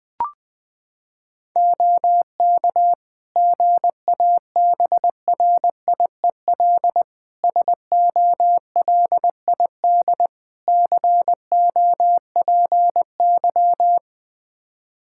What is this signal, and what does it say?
Morse code, heard as a keyed tone, about 20 wpm, 700 Hz. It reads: OK GABRIEL SOLID COPY
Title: Practice QSO - Exchange 3 - More Info - 20wpm